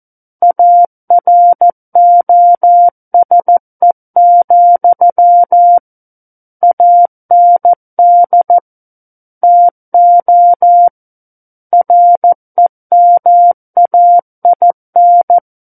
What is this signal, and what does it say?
Morse code, heard as a keyed tone, about 14 wpm, 700 Hz. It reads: AROSE, AND TO REMAIN
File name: Book One / 05-Chapter 5 - 14 WPM